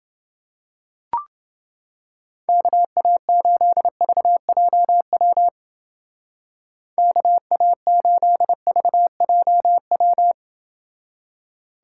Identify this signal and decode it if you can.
Morse code, heard as a keyed tone, about 29 wpm, 700 Hz. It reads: KA8VJW KA8VJW